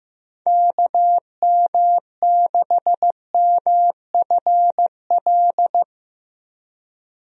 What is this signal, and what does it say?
Morse code, heard as a keyed tone, about 15 wpm, 700 Hz. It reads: KM6MFL
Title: USA Call Signs - 6 Characters - Repeated 2x - 15wpm